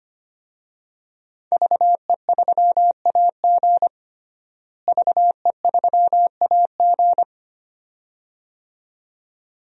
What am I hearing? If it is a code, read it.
Morse code, heard as a keyed tone, about 25 wpm, 700 Hz. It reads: VE3AG VE3AG
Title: DXCC-100 Call Signs - 5 Characters - Repeated 2x - 25wpm